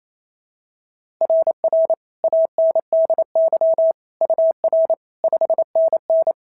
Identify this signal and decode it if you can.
Morse code, heard as a keyed tone, about 25 wpm, 650 Hz. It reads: RR ANDY UR 5NN
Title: SOTA QSO Practice 28wpm